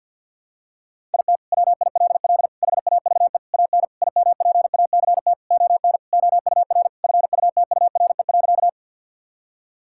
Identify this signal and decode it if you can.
Morse code, heard as a keyed tone, about 49 wpm, 700 Hz. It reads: IT WILL HAVE AN IMPACT ON OUR FUTURE.
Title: Sentences from Top 1000 Words - Review Part 6 - 50wpm